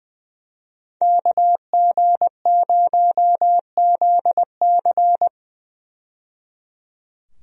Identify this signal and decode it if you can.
Morse code, heard as a keyed tone, about 20 wpm, 700 Hz. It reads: KG0ZC
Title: USA Call Signs - 5 Characters - 20wpm